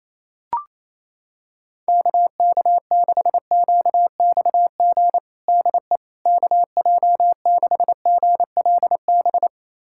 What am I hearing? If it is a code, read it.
Morse code, heard as a keyed tone, about 28 wpm, 700 Hz. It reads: KK6QXG DE KJ6GLB
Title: USA Callsign DE Callsign - 6 Characters - 28wpm